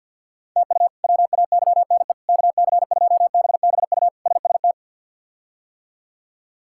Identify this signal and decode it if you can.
Morse code, heard as a keyed tone, about 52 wpm, 690 Hz. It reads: TU WAYNE KC2BBU SST
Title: K1USN SST QSO Practice - 50wpm